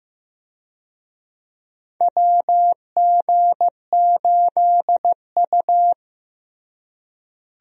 Morse code, heard as a keyed tone, about 15 wpm, 700 Hz. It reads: WG8U
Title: USA Call Signs - 4 Characters - Repeated 2x - 15wpm